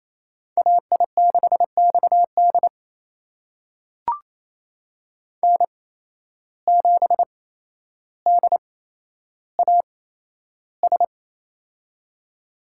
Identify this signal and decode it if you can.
Morse code, heard as a keyed tone, about 27 wpm, spaced as for 12 wpm, 700 Hz. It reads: AI6XD N7DAS
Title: USA Call Signs - 5 Characters - 8x Character Spacing - 28wpm